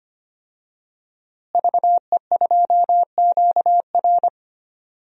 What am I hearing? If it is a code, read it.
Morse code, heard as a keyed tone, about 25 wpm, 700 Hz. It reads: VE2QR
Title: DXCC-100 Call Signs - 5 Characters - Repeated 3x - 25wpm